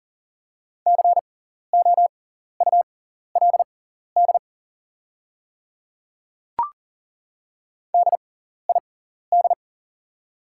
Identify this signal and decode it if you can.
Morse code, heard as a keyed tone, about 40 wpm, spaced as for 14 wpm, 700 Hz. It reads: COULD DID
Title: 6x Character Spacing - Top 300 Words - 40wpm